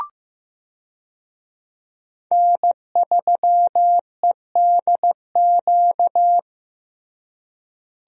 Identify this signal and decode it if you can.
Morse code, heard as a keyed tone, about 15 wpm, 700 Hz. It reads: N3EDQ
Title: USA Call Signs - 5 Characters - Repeated 2x - 15wpm